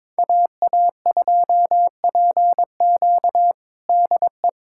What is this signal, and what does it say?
Morse code, heard as a keyed tone, about 22 wpm, 700 Hz. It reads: AA2PQ DE
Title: USA Callsign DE Callsign 22wpm